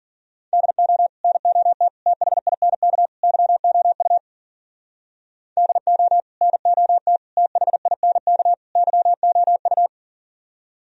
Morse code, heard as a keyed tone, about 53 wpm, 700 Hz. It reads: DO NOT THINK YOU DO NOT THINK YOU